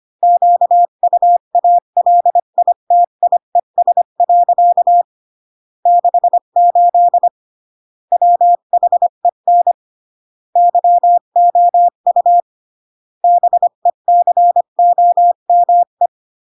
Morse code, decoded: QUALITIES. 68 WHEN YOU BECOME